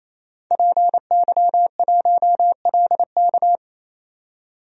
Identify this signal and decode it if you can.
Morse code, heard as a keyed tone, about 28 wpm, 700 Hz. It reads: PY1LK
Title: DXCC-100 Call Signs - 5 Characters - 28wpm